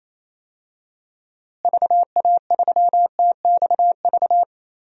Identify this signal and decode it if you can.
Morse code, heard as a keyed tone, about 28 wpm, 700 Hz. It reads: VA3TXV
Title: DXCC-100 Call Signs - 6 Characters - Repeated 2x - 28wpm